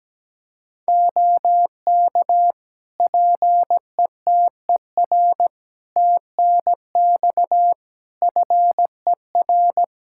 Morse code, heard as a keyed tone, about 17 wpm, 700 Hz. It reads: OK PETER TNX FER
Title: K1USN SST QSO Practice 17wpm